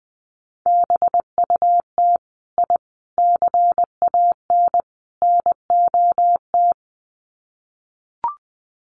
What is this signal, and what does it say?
Morse code, heard as a keyed tone, about 20 wpm, 700 Hz. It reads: BUT I CAN NOT
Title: Sets of 4 Words 20wpm